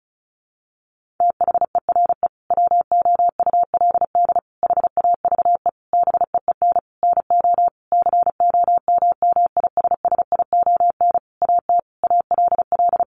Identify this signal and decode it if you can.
Morse code, heard as a keyed tone, about 34 wpm, 700 Hz. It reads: THERE WOULD HAVE BEEN NO COMMISSION AT ALL
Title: Sentences from Top 1400 Words 35wpm